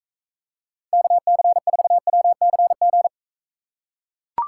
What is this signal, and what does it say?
Morse code, heard as a keyed tone, about 43 wpm, 700 Hz. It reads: KC4WCG